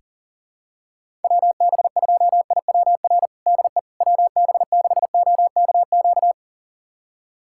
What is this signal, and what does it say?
Morse code, heard as a keyed tone, about 38 wpm, 700 Hz. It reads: WB2IWR DE WB6OKQ
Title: USA Callsign DE Callsign 40wpm